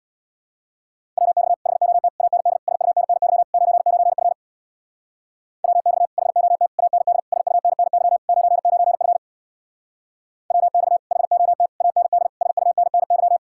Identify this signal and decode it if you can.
Morse code, heard as a keyed tone, about 64 wpm, 700 Hz. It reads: WX HOT AND SUNNY 99F WX HOT AND SUNNY 99F WX HOT AND SUNNY